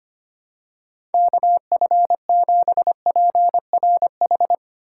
Morse code, heard as a keyed tone, about 25 wpm, 700 Hz. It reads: KF7PRH
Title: USA Call Signs - 4x Character Spacing - 25wpm